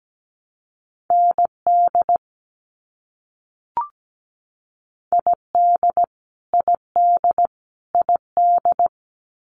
Morse code, heard as a keyed tone, about 17 wpm, 700 Hz. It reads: ND ID ID ID
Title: US States Repeated 3x 17wpm